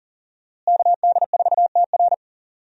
Morse code, heard as a keyed tone, about 40 wpm, 700 Hz. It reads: KD4TR
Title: USA Call Signs - 5 Characters - Repeated 3x - 40wpm